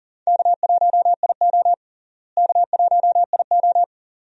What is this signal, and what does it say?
Morse code, heard as a keyed tone, about 40 wpm, 700 Hz. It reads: K1IO K1IO
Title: USA Call Signs - 4 Characters - Repeated 2x - 40wpm